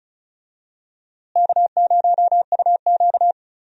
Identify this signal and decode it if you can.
Morse code, heard as a keyed tone, about 36 wpm, 700 Hz. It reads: K0UQ
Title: USA Call Signs - 4 Characters - Repeated 3x - 35wpm